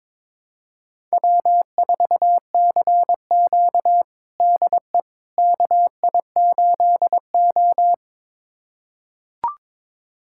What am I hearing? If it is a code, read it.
Morse code, heard as a keyed tone, about 22 wpm, 700 Hz. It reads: W4CQ DE KI8O
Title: USA Callsign DE Callsign - 4 Characters - 22wpm